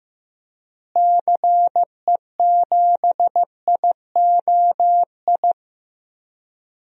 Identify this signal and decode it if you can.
Morse code, heard as a keyed tone, about 15 wpm, 700 Hz. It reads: CE7IOI
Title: DXCC-100 Call Signs - 6 Characters - Repeated 3x - 15wpm